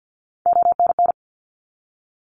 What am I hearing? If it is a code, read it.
Morse code, heard as a keyed tone, about 50 wpm, 700 Hz. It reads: ONN
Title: ARRL Sections 50wpm